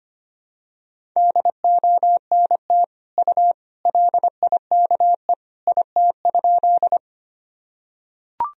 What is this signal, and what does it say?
Morse code, heard as a keyed tone, about 25 wpm, 700 Hz. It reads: DONT U LIKE IT?
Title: Sentences from Top 300 Words and Abbreviations 25wpm